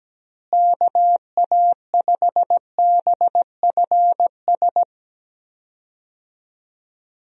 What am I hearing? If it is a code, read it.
Morse code, heard as a keyed tone, about 17 wpm, 700 Hz. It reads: KA5BFS
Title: USA Call Signs - 6 Characters - Repeated 2x - 17wpm